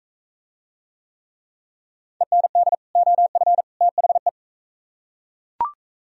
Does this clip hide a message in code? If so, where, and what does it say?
Yes: Morse code from 2.2 to 4.3 s, heard as a keyed tone, about 40 wpm, 700 Hz. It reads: END OF THE